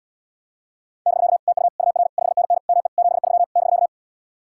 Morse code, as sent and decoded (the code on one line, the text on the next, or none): -...-.- / - ..- / ..- .-. / ..... -. -. / -- . / --... ...-- / -...-.-
<BK> TU UR 5NN ME 73 <BK>